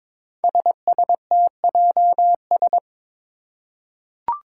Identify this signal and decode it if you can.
Morse code, heard as a keyed tone, about 22 wpm, 700 Hz. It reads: SSTJS
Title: Five Challenging Letter-Number Combinations 22wpm